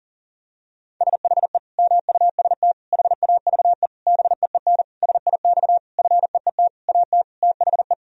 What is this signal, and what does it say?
Morse code, heard as a keyed tone, about 40 wpm, 700 Hz. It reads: SHE MUST HAVE BEEN SIX FEET AT THE
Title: Sentences from Top 1000 Words - Review Part 2 - 40wpm